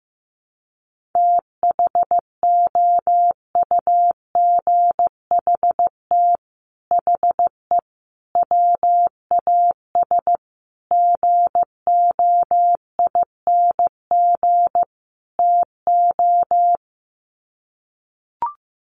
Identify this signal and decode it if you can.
Morse code, heard as a keyed tone, about 15 wpm, 700 Hz. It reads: THOUGHT HE WAS GOING TO
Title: Sets of 5 Words Encore2 15wpm